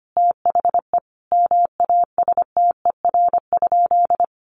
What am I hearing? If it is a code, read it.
Morse code, heard as a keyed tone, about 25 wpm, 700 Hz. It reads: THE MASTER?
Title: Sentences from Top 1300 Words 25wpm